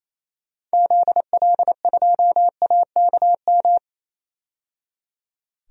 Morse code, heard as a keyed tone, about 28 wpm, 700 Hz. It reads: ZL2AKM
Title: DXCC-100 Call Signs - 6 Characters - 28wpm